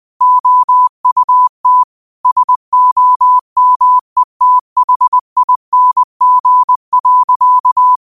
Morse code, heard as a keyed tone, about 20 wpm, 1000 Hz. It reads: OUT SOMETHING.
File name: qotdfiles / 20wpm